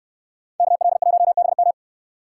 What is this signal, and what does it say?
Morse code, heard as a keyed tone, about 69 wpm, 700 Hz. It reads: KL1BM